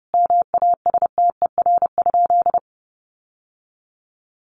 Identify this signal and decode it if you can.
Morse code, heard as a keyed tone, about 30 wpm, 700 Hz. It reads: MASTER?
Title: Sentences from Top 1300 Words 30wpm